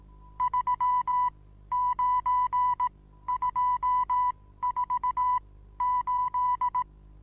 Morse code, heard as a keyed tone, about 18 wpm, spaced as for 13 wpm, 990 Hz. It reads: 39248
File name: Onlinelehrgang / morsen / DJ4UF Morsen25